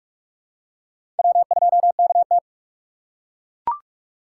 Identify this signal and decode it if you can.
Morse code, heard as a keyed tone, about 41 wpm, 700 Hz. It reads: W2KT
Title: USA Call Signs - 4 Characters - 6x Character Spacing - 45wpm